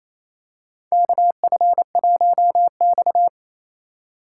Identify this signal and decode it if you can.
Morse code, heard as a keyed tone, about 27 wpm, 700 Hz. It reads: KF1X
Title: USA Call Signs - 8x Character Spacing - 28wpm